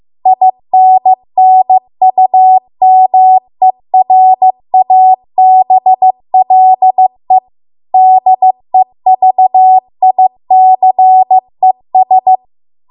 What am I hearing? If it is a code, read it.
Morse code, heard as a keyed tone, about 15 wpm, 750 Hz. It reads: INNUMERABLE DEVICES
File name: Archive / 15 WPM / 15WPM